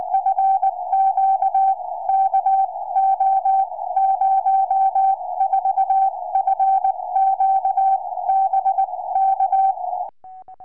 Morse code, heard as a keyed tone, about 20 wpm, spaced as for 16 wpm, 770 Hz. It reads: FQ KO04 FQ BK
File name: tropo / ua2fl 2m